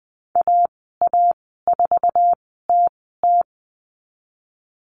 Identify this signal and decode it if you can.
Morse code, heard as a keyed tone, about 20 wpm, spaced as for 15 wpm, 700 Hz. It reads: AA4TT
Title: 076 Lesson 13g - Letter L - Call Signs 20wpm